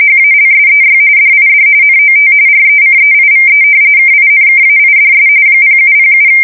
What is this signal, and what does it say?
RTTY, about 45.45 baud. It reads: HALSEY DID NOT RECEIVE KINKAIDS 0707
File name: fsk170 / battleshipnj